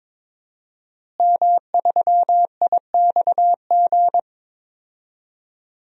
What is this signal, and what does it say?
Morse code, heard as a keyed tone, about 22 wpm, 700 Hz. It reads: M3IXG
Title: DXCC-100 Call Signs - 5 Characters - Repeated 3x - 22wpm